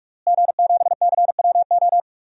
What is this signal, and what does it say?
Morse code, heard as a keyed tone, about 46 wpm, 700 Hz. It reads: G7CWO
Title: DXCC-100 Call Signs - 5 Characters - Repeated 2x - 45wpm